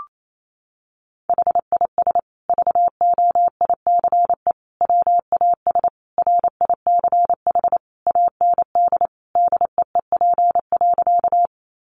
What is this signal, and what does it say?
Morse code, heard as a keyed tone, about 28 wpm, 700 Hz. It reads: HIS VOICE WAS RICH AND DEEP.
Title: Sentences from Top 1100 Words 28wpm